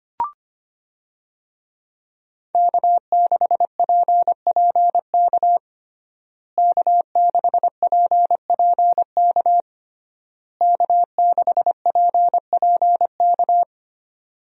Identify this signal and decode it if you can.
Morse code, heard as a keyed tone, about 25 wpm, 700 Hz. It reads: K6PPK K6PPK K6PPK
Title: USA Call Signs - 5 Characters - Repeated 3x - 25wpm